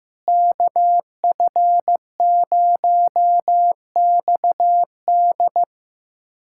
Morse code, heard as a keyed tone, about 15 wpm, 700 Hz. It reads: KF0XD
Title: USA Callsign DE Callsign - 5 Characters - 15wpm